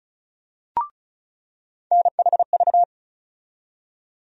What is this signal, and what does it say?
Morse code, heard as a keyed tone, about 34 wpm, 700 Hz. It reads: NHV
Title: USA Call Sign Postfixes - 3 Letters 35wpm